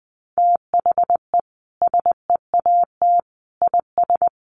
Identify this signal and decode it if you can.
Morse code, heard as a keyed tone, about 20 wpm, 700 Hz. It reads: THE SEAT IS
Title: Sentences from Top 1500 Words 20wpm